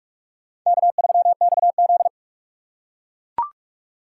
Morse code, heard as a keyed tone, about 41 wpm, 690 Hz. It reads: K3XZ
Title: USA Call Signs - 4 Characters - 3x Character Spacing - 45wpm